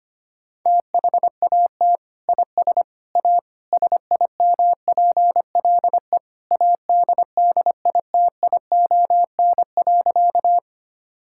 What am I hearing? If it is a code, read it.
Morse code, heard as a keyed tone, about 25 wpm, 700 Hz. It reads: THAT IS A SIMPLE ADDITION.
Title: Sentences from Top 1000 Words - Review Part 6 - 25wpm